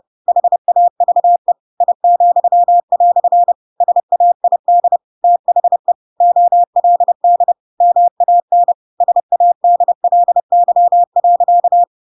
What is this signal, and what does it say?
Morse code, heard as a keyed tone, about 30 wpm, 700 Hz. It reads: HAVE I," SAID THE OLD MAN SADLY.
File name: Pinocchio / Pinocchio ch8 30 30 ik1hge